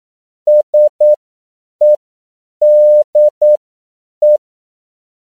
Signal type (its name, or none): Morse code